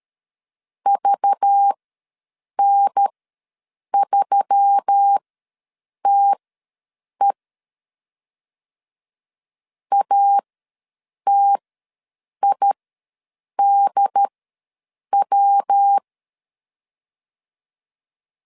Morse code, heard as a keyed tone, about 13 wpm, spaced as for 7 wpm, 800 Hz. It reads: VN3TE ATIDW
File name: LEZIONI / 5 LBVC / 5 LBV ws 12wpm 3015